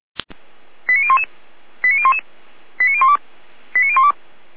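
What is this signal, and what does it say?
Five-tone selective call (ZVEI), heard as quick runs of five tones, tones of about 70 ms. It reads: 89011 89011 89012 89012
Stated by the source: ZVEI five-tone selective call received on SDR, from the Feuerwehr Orlamünde fire brigade server.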